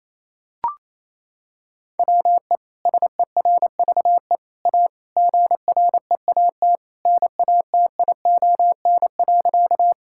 Morse code, heard as a keyed tone, about 28 wpm, 700 Hz. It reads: WE SERVE A GREAT NATION.
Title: Sentences from Top 1000 Words - Review Part 6 - 28wpm